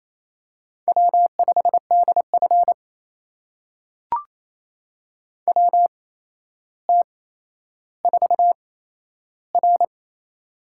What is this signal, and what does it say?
Morse code, heard as a keyed tone, about 28 wpm, spaced as for 12 wpm, 700 Hz. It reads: W5DF WT4R
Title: USA Call Signs - 4 Characters - 8x Character Spacing - 28wpm